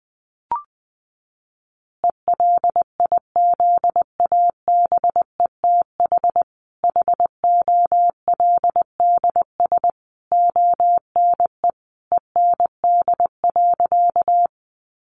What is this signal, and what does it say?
Morse code, heard as a keyed tone, about 20 wpm, 700 Hz. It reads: ELIZABETH HOLDS ONE END.
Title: Easy Phrases 20wpm